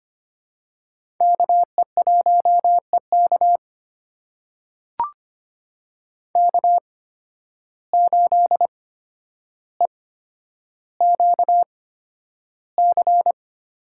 Morse code, heard as a keyed tone, about 25 wpm, spaced as for 11 wpm, 700 Hz. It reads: KE1EK K8EQC
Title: USA Call Signs - 5 Characters - 8x Character Spacing - 25wpm